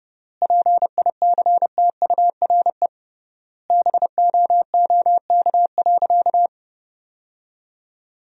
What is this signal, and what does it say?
Morse code, heard as a keyed tone, about 30 wpm, 700 Hz. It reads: PICTURE BOOK.
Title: 3x Word-Spacing - Sentences from Top 500 Words - 30wpm